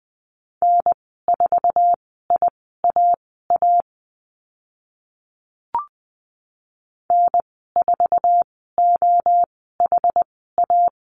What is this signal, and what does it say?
Morse code, heard as a keyed tone, about 19 wpm, spaced as for 14 wpm, 700 Hz. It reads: N4IAA N4OHA
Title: 062 Lesson 11g - Letter h - Call Signs 20wpm